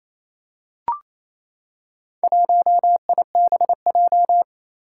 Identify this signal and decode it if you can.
Morse code, heard as a keyed tone, about 27 wpm, 700 Hz. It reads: 1IBJ
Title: Four Challenging Letter-Number Combinations 28wpm